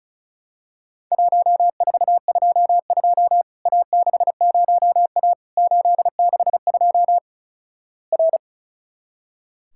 Morse code, heard as a keyed tone, about 36 wpm, 700 Hz. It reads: 1422 A60A 862 R
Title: Worked All Europe DX Contest - QTC Exchange - 35wpm